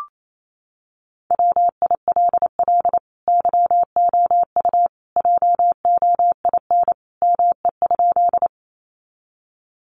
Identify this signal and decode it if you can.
Morse code, heard as a keyed tone, about 28 wpm, 700 Hz. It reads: WILL YOU JOIN ME?
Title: Sentences from Top 1400 Words 28wpm